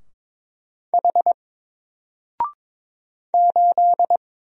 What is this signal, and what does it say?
Morse code, heard as a keyed tone, about 22 wpm, 700 Hz. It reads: H 8
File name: Single-Letter-Number-22wpm